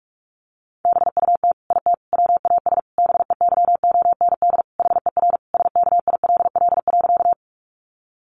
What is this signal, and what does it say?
Morse code, heard as a keyed tone, about 44 wpm, 700 Hz. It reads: BUT IT WAS BEYOND HER SKILL.
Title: Sentences from Top 1200 Words 45wpm